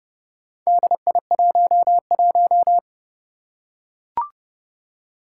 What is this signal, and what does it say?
Morse code, heard as a keyed tone, about 30 wpm, 700 Hz. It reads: DI11
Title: Four Challenging Letter-Number Combinations 30wpm